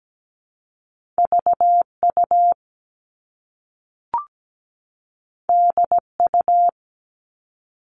Morse code, heard as a keyed tone, about 17 wpm, 700 Hz. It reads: VU DU
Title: All Cut Numbers - 2-Digit 17wpm